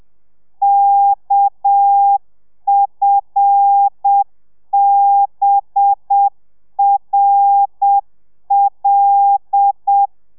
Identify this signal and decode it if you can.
Morse code, heard as a keyed tone, about 7 wpm, 800 Hz. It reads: KFBRL